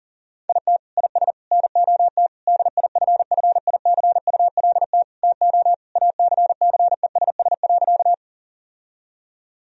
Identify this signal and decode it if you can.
Morse code, heard as a keyed tone, about 37 wpm, 700 Hz. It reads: IT IS NOT DIFFICULT TO ACCESS.
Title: Sentences from Top 1000 Words - Review Part 4 - 40wpm